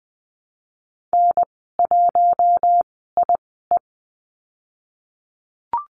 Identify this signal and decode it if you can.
Morse code, heard as a keyed tone, about 20 wpm, spaced as for 14 wpm, 700 Hz. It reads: N1IE
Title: 042 Lesson 8g - Number 1 - Call Signs 20wpm